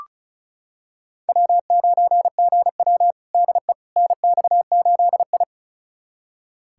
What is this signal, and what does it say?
Morse code, heard as a keyed tone, about 36 wpm, 700 Hz. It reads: W9GW DE NX8I
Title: USA Callsign DE Callsign - 4 Characters - 35wpm